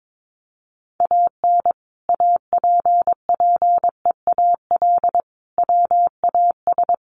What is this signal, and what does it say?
Morse code, heard as a keyed tone, about 22 wpm, 700 Hz. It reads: AN APPEAL WAS
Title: Sentences from Top 1300 Words 22wpm